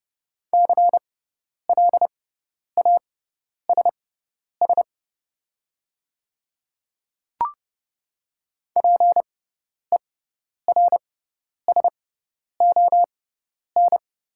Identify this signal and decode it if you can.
Morse code, heard as a keyed tone, about 31 wpm, spaced as for 10 wpm, 690 Hz. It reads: CLASS PERSON
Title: 6x Character Spacing - Top 500 Words - 30wpm